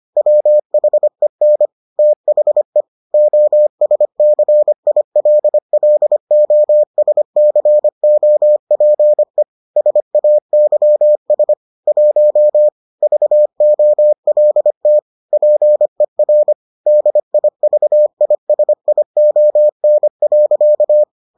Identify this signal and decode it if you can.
Morse code, heard as a keyed tone, about 25 wpm, 600 Hz. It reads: WHEN THE OSCILLOSCOPE SAYS 1 VOLT PER DIVISION.